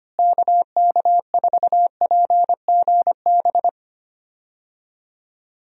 Morse code, heard as a keyed tone, about 25 wpm, 700 Hz. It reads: KK4PGB
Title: USA Call Signs - 6 Characters - Repeated 2x - 25wpm